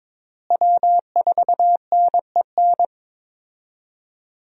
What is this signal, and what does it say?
Morse code, heard as a keyed tone, about 22 wpm, 700 Hz. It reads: W4NEN
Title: USA Call Signs - 5 Characters - Repeated 2x - 22wpm